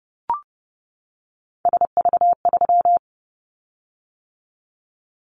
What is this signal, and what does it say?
Morse code, heard as a keyed tone, about 30 wpm, 700 Hz. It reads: SV3
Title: Callsign Prefixes for Top 50 DXCC - 30wpm